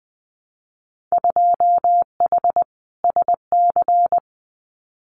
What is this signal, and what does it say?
Morse code, heard as a keyed tone, about 20 wpm, 700 Hz. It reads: 2H SC
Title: Winter Field Day Exchanges 20wpm